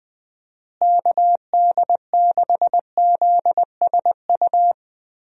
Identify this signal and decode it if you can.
Morse code, heard as a keyed tone, about 20 wpm, 700 Hz. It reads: KD6ZSU
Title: USA Call Signs - 6 Characters - Repeated 2x - 20wpm